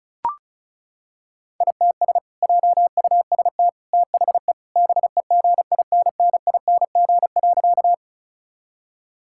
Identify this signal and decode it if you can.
Morse code, heard as a keyed tone, about 36 wpm, 700 Hz. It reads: ITS JUST THE BEGINNING.
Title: Sentences from Top 1000 Words - Review Part 5 - 35wpm